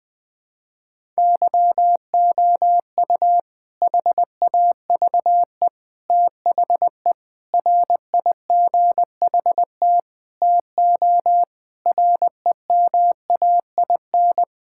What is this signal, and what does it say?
Morse code, heard as a keyed tone, about 20 wpm, 700 Hz. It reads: YOU HAVE THE RIGHT TO REMAIN
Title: Sentences with 4-Word Phrases - 20wpm